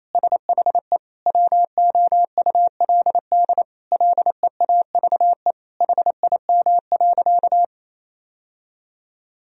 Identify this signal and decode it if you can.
Morse code, heard as a keyed tone, about 28 wpm, 700 Hz. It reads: SHE WOULD LEAVE HIM.
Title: Sentences from Top 1000 Words - Review Part 4 - 28wpm